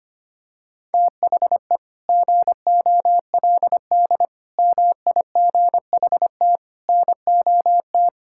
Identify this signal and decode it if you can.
Morse code, heard as a keyed tone, about 25 wpm, 700 Hz. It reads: THE GOLD MIGHT NOT
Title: Sentences from Top 1000 Words - Review Part 5 - 25wpm